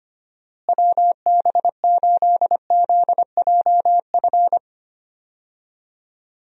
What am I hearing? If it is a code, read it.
Morse code, heard as a keyed tone, about 25 wpm, 700 Hz. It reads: WB8ZJF
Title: USA Call Signs - 6 Characters - Repeated 3x - 25wpm